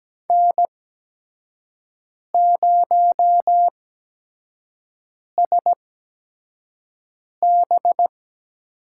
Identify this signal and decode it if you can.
Morse code, heard as a keyed tone, about 17 wpm, spaced as for 5 wpm, 700 Hz. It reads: N0SB
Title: USA Call Signs - 5 Characters - 8x Character Spacing - 17wpm